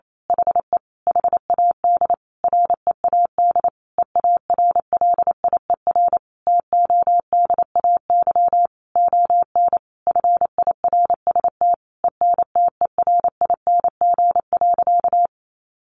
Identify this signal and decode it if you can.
Morse code, heard as a keyed tone, about 28 wpm, 700 Hz. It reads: HE HAD READ EARLIER TODAY ON FIRST ENTERING.
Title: A Sound Of Thunder 28wpm